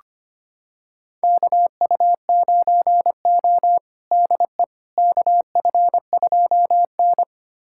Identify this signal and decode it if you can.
Morse code, heard as a keyed tone, about 25 wpm, 700 Hz. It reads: KU9O DE KF2N
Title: USA Callsign DE Callsign - 4 Characters - 25wpm